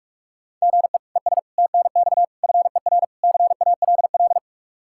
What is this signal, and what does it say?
Morse code, heard as a keyed tone, about 46 wpm, 700 Hz. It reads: GE ES TNX FER CALL
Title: Practice QSO - Exchange 1 - Beginning - 45wpm